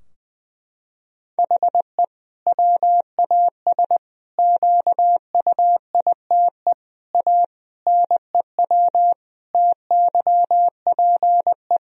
Morse code, heard as a keyed tone, about 20 wpm, 700 Hz. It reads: HE WAS QUITE A NEW TYPE
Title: Sentences from Top 1000 Words - Review Part 3 - 20wpm